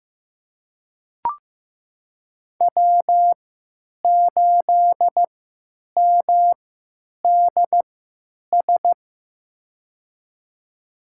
Morse code, heard as a keyed tone, about 16 wpm, spaced as for 9 wpm, 700 Hz. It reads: W8MDS
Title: USA Call Signs - 5 Characters - 3x Character Spacing - 15wpm